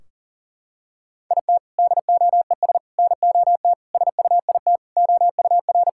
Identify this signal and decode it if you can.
Morse code, heard as a keyed tone, about 40 wpm, 700 Hz. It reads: IT DOES NOT SUIT OUR
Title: Sentences from Top 1600 Words 40wpm